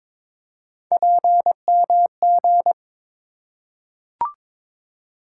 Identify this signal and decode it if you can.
Morse code, heard as a keyed tone, about 22 wpm, 700 Hz. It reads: PMG
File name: practice / 22wpm / Three Letter-Number Combinations - 22wpm